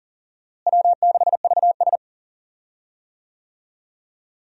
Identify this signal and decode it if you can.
Morse code, heard as a keyed tone, about 40 wpm, 700 Hz. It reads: W6VS
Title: USA Call Signs - 4 Characters - Repeated 3x - 40wpm